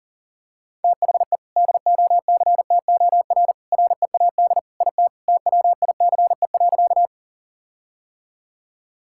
Morse code, heard as a keyed tone, about 38 wpm, 700 Hz. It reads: THE DOCTOR READ IT TWICE.
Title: Sentences from Top 1600 Words 40wpm